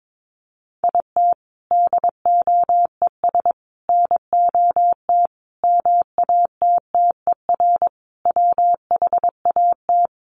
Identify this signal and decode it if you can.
Morse code, heard as a keyed tone, about 22 wpm, 700 Hz. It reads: IT DOES NOT MATTER WHAT
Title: Sets of 5 Words Encore 22wpm